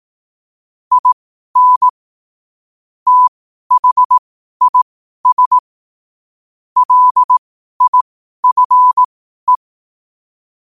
Morse code, heard as a keyed tone, about 18 wpm, spaced as for 12 wpm, 1000 Hz. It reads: IN THIS LIFE
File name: qotdfiles / 12wpm